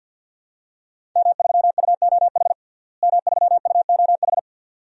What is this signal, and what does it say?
Morse code, heard as a keyed tone, about 45 wpm, 700 Hz. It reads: M3UOH M3UOH
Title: DXCC-100 Call Signs - 5 Characters - Repeated 2x - 50wpm